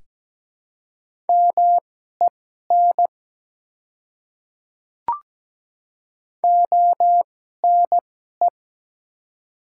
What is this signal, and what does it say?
Morse code, heard as a keyed tone, about 17 wpm, spaced as for 12 wpm, 700 Hz. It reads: MEN ONE
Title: 2x Character Spacing - Top 200 Words - 17wpm